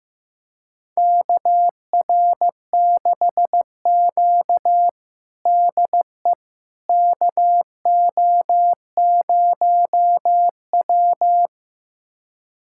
Morse code, heard as a keyed tone, about 15 wpm, 700 Hz. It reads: KR6Q DE KO0W
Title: USA Callsign DE Callsign - 4 Characters - 15wpm